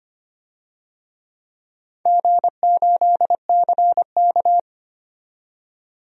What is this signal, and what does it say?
Morse code, heard as a keyed tone, about 25 wpm, 700 Hz. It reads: G8CK
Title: DXCC-100 Call Signs - 4 Characters - 25wpm